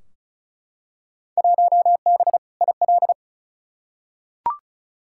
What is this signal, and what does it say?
Morse code, heard as a keyed tone, about 36 wpm, 700 Hz. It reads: 1B IL
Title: Field Day Exchanges Encore - 35wpm